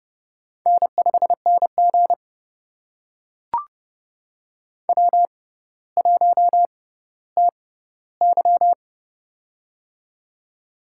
Morse code, heard as a keyed tone, about 30 wpm, spaced as for 15 wpm, 700 Hz. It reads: N5NG W1TY
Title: USA Call Signs - 4 Characters - 6x Character Spacing - 30wpm